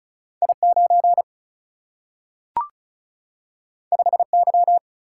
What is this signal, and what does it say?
Morse code, heard as a keyed tone, about 36 wpm, 700 Hz. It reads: I9 5Y
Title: Single-Letter Single-Number 35wpm